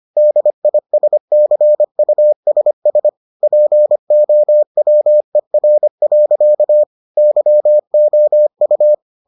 Morse code, heard as a keyed tone, about 25 wpm, 600 Hz. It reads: DISCUSS POWER. YOU